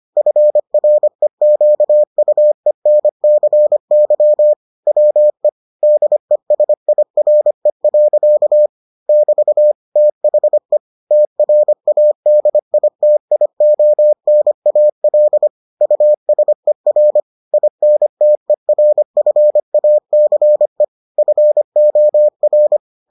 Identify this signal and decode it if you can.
Morse code, heard as a keyed tone, about 25 wpm, 600 Hz. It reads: FREQUENCY WE DESIRE. <BT> THE TRADITIONAL USER INTERFACE FOR